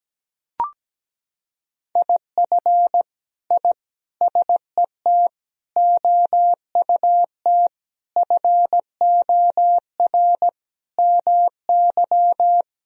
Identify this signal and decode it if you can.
Morse code, heard as a keyed tone, about 17 wpm, 700 Hz. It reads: IF I SET OUT FOR MY
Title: Sentences from Top 1000 Words - Review Part 2 - 17wpm